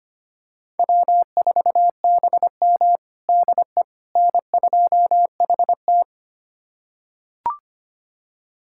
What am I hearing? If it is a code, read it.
Morse code, heard as a keyed tone, about 25 wpm, 700 Hz. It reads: W4BM DE N2HT
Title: USA Callsign DE Callsign - 4 Characters - 25wpm